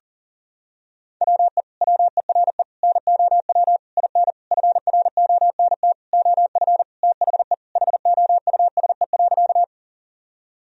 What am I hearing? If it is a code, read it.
Morse code, heard as a keyed tone, about 37 wpm, 700 Hz. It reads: WE WERE NOW IN FRONT OF THE HOUSE.
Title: Sentences from Top 1000 Words - Review Part 2 - 40wpm